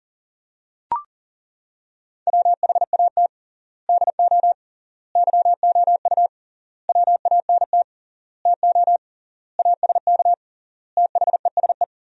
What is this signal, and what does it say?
Morse code, heard as a keyed tone, about 37 wpm, 700 Hz. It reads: WHAT DO YOU WANT TO ASK THESE
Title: 3x Word-Spacing - Sentences from Top 200 Words - 40wpm